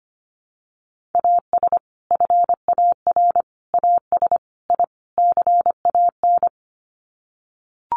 Morse code, heard as a keyed tone, about 25 wpm, 700 Hz. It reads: AS FAR AS I CAN
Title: Sets of 5 Words Encore 25wpm